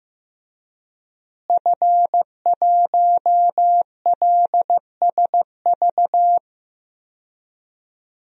Morse code, heard as a keyed tone, about 15 wpm, 700 Hz. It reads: F1LSV
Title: DXCC-100 Call Signs - 5 Characters - Repeated 2x - 15wpm